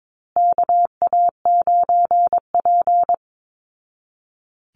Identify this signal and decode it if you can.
Morse code, heard as a keyed tone, about 22 wpm, 700 Hz. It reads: KA9P
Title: Call Signs Repeated 2x - 22wpm